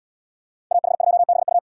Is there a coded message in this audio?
Yes: Morse code from 0.7 to 1.6 s, heard as a keyed tone, about 72 wpm, 700 Hz. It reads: AH2BU